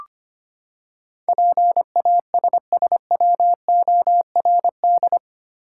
Morse code, heard as a keyed tone, about 25 wpm, 700 Hz. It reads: PASSWORD
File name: Eight Letter Words - 25wpm